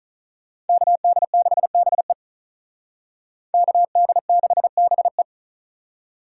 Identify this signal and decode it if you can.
Morse code, heard as a keyed tone, about 45 wpm, 700 Hz. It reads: KD6BE KD6BE